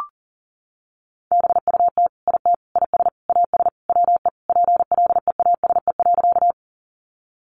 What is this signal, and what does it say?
Morse code, heard as a keyed tone, about 40 wpm, 700 Hz. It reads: BUT IT IS AS WE PLEASE.
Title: Sentences from Top 1500 Words 40wpm